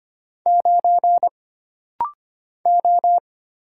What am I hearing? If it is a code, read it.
Morse code, heard as a keyed tone, about 25 wpm, 700 Hz. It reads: 9 O
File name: Single Character (Letter:Number:Punctuation:Prosign) - Rapid-Fire - 25wpm